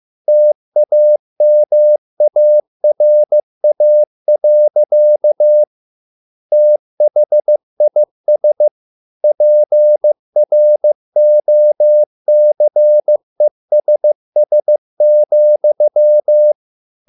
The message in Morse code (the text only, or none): TAMARA. THIS PROCESS,